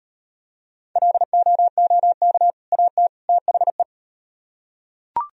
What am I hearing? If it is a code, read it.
Morse code, heard as a keyed tone, about 38 wpm, 700 Hz. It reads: LOOK AT THE